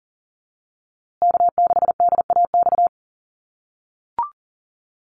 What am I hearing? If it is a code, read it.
Morse code, heard as a keyed tone, about 40 wpm, 700 Hz. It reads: K6DAX
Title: USA Call Signs Repeated 3x 40wpm